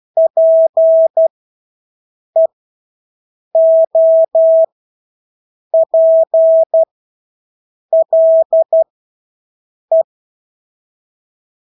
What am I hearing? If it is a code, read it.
Morse code, heard as a keyed tone, about 12 wpm, spaced as for 6 wpm, 650 Hz. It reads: PEOPLE
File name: Engelske ord 12-6 wpm